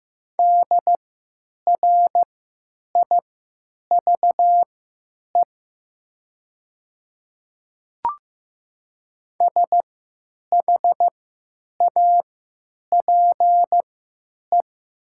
Morse code, heard as a keyed tone, about 15 wpm, spaced as for 9 wpm, 700 Hz. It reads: DRIVE SHAPE
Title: 3x Character Spacing - Top 500 Words - 15wpm